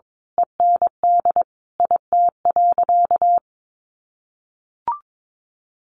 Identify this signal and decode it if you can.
Morse code, heard as a keyed tone, about 22 wpm, 700 Hz. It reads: END IT.
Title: Sentences from Top 1200 Words 22wpm